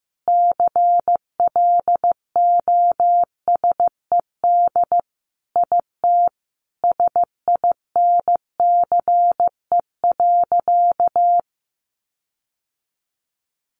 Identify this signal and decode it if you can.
Morse code, heard as a keyed tone, about 15 wpm, 700 Hz. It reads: CLOSED IT SINCE.
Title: Sentences from Top 1400 Words 15wpm